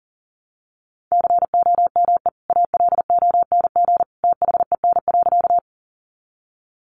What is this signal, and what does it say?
Morse code, heard as a keyed tone, about 40 wpm, 700 Hz. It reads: COME ALONG THEN.
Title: Sentences from Top 700 Words 40wpm